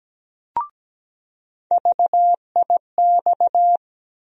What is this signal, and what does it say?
Morse code, heard as a keyed tone, about 17 wpm, 700 Hz. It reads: VIX
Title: USA Call Sign Postfixes - 3 Letters 17wpm